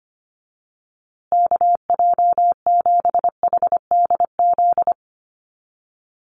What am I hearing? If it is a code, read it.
Morse code, heard as a keyed tone, about 25 wpm, 700 Hz. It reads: KJ7HDZ
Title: USA Call Signs Repeated 3x 25wpm